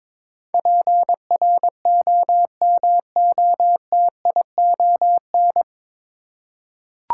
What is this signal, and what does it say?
Morse code, heard as a keyed tone, about 22 wpm, 700 Hz. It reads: PROMOTION
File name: 9 Letter Words - First 3 Letters Given - 22wpm